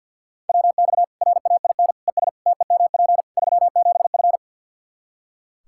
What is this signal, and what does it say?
Morse code, heard as a keyed tone, about 45 wpm, 700 Hz. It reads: WX RAIN ES TEMP 37F
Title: Practice QSO - Exchange 2 - Info - 50wpm